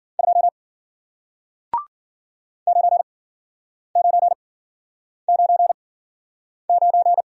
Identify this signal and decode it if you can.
Morse code, heard as a keyed tone, about 51 wpm, spaced as for 9 wpm, 700 Hz. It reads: 2 9999